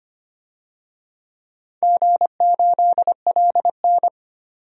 Morse code, heard as a keyed tone, about 25 wpm, 700 Hz. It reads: G8LN
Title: DXCC-100 Call Signs - 4 Characters - Repeated 2x - 25wpm